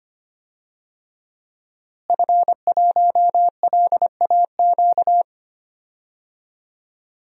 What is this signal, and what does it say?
Morse code, heard as a keyed tone, about 25 wpm, 700 Hz. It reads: F1LAQ
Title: DXCC-100 Call Signs - 5 Characters - Repeated 2x - 25wpm